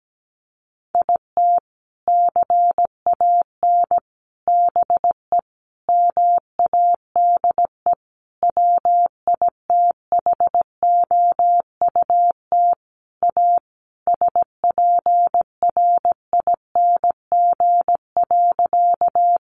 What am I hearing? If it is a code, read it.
Morse code, heard as a keyed tone, about 17 wpm, 700 Hz. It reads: IT CAN BE MADE WITHOUT A SPRING.
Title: Sentences from Top 1200 Words 17wpm